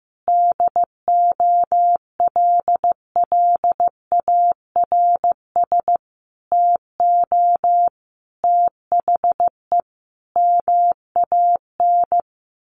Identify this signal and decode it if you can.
Morse code, heard as a keyed tone, about 15 wpm, 700 Hz. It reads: DOLLARS TO THE MAN
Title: A Sound Of Thunder 15wpm